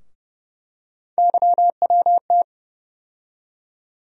Morse code, heard as a keyed tone, about 30 wpm, 700 Hz. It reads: YWT
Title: USA Call Sign Postfixes - 3 Letters 30wpm